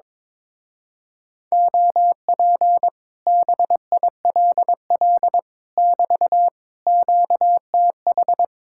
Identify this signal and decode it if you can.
Morse code, heard as a keyed tone, about 22 wpm, 700 Hz. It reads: OP BILL <BT> QTH
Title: Speed-Racing Common QSO Phrases - 22wpm